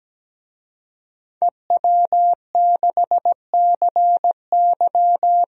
Morse code, heard as a keyed tone, about 17 wpm, 700 Hz. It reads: EW6CY
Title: DXCC-100 Call Signs - 5 Characters - 17wpm